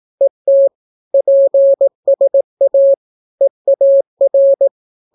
Morse code, heard as a keyed tone, about 18 wpm, 550 Hz. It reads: ET PSA EAR